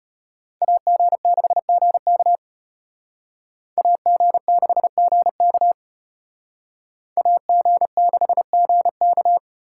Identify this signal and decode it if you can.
Morse code, heard as a keyed tone, about 32 wpm, 700 Hz. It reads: AG6GK AG6GK AG6GK